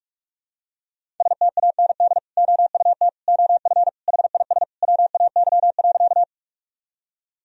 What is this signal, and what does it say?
Morse code, heard as a keyed tone, about 42 wpm, 700 Hz. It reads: STAND OUT OF HIS WAY.
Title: Sentences from Top 1000 Words - Review Part 1 - 45wpm